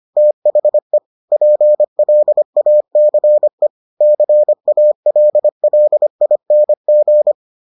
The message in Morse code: THE PLACE CALLING

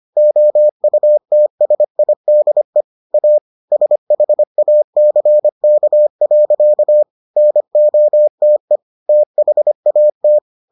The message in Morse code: OUTSIDE A SHACK. NOTE THAT

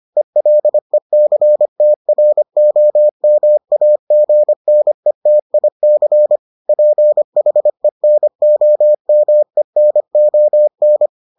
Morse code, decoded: ELECTROMAGNETIC PHENOMENON

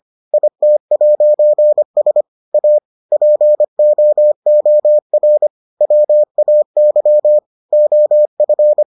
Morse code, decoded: IT'S A POOR WAY OF

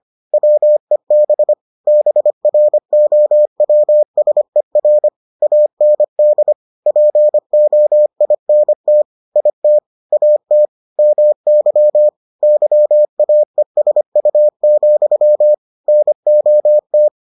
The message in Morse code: WEB BROWSER AND POINT IT AT MY YAESU, NOT

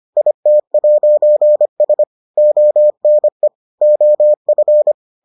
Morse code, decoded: IT'S ONE OF

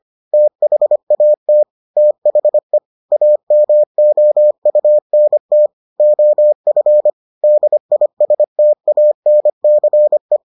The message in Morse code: THAT THE AMOUNT OF DISTANCE

